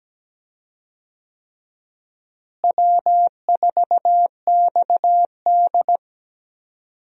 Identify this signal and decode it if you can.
Morse code, heard as a keyed tone, about 17 wpm, 700 Hz. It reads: W4XD
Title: USA Call Signs - 4 Characters - Repeated 3x - 17wpm